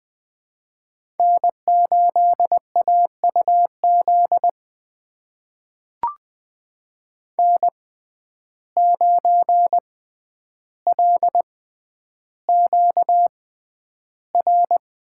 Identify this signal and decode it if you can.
Morse code, heard as a keyed tone, about 20 wpm, spaced as for 10 wpm, 700 Hz. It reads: N8AUZ N9LQR
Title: USA Call Signs - 5 Characters - 6x Character Spacing - 20wpm